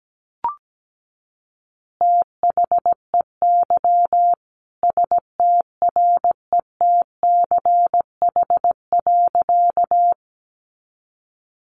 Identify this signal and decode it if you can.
Morse code, heard as a keyed tone, about 17 wpm, 700 Hz. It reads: THEY STRETCH.
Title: Easy Phrases 17wpm